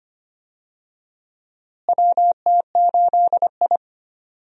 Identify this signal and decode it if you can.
Morse code, heard as a keyed tone, about 25 wpm, 700 Hz. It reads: WT8I
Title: USA Call Signs - 4 Characters - Repeated 2x - 25wpm